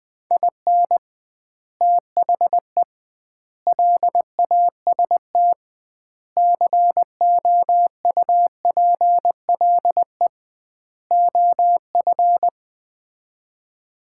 Morse code, heard as a keyed tone, about 20 wpm, 700 Hz. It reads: IN THE LAST COUPLE OF
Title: 2x Word-Spacing - Sets of 5 Words - 20wpm